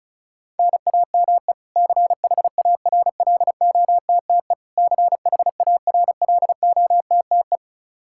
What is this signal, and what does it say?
Morse code, heard as a keyed tone, about 34 wpm, 700 Hz. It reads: NAME CHARLOTTE CHARLOTTE
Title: Practice QSO - Exchange 1 - Beginning - 35wpm